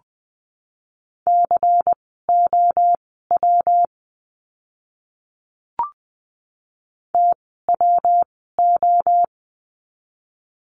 Morse code, heard as a keyed tone, about 20 wpm, spaced as for 14 wpm, 700 Hz. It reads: COW TWO
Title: 115 Lesson 19f - Letter w - Words 20wpm